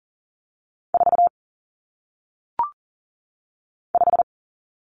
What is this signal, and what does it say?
Morse code, heard as a keyed tone, about 39 wpm, 700 Hz. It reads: 4 5
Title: Numbers - 1-Digit 40wpm